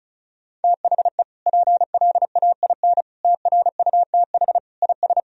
Morse code, heard as a keyed tone, about 34 wpm, 700 Hz. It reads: THE PLAIN TRUTH IS
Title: Sentences from Top 1000 Words - Review Part 2 - 35wpm